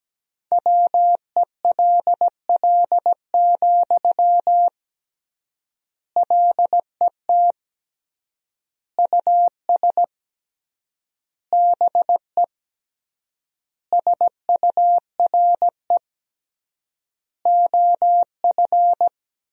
Morse code, heard as a keyed tone, about 17 wpm, 700 Hz. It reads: WELL, LET US BE SURE OF
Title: 3x Word-Spacing - Sentences from Top 300 Words - 17wpm